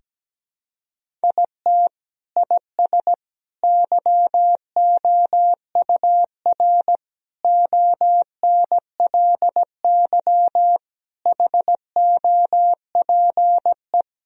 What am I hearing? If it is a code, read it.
Morse code, heard as a keyed tone, about 17 wpm, 700 Hz. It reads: IT IS YOUR ONLY HOPE
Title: Sentences from Top 1000 Words - Review Part 4 - 17wpm